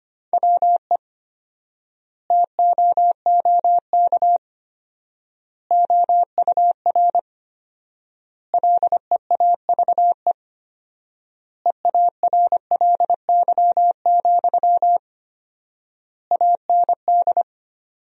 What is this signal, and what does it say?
Morse code, heard as a keyed tone, about 25 wpm, 700 Hz. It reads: WE TOOK OUR LEAVE EARLY, AND
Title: 4x Word-Spacing - Sentences from Top 400 Words - 25wpm